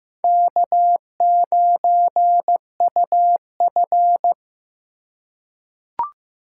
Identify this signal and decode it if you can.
Morse code, heard as a keyed tone, about 15 wpm, 700 Hz. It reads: K9UF
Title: USA Call Signs - 3x Character Spacing - 15wpm